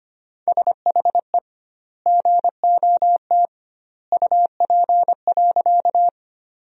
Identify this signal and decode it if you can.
Morse code, heard as a keyed tone, about 25 wpm, 700 Hz. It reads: SHE GOT UP.
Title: 2x Word-Spacing - Sentences from Top 300 Words - 25wpm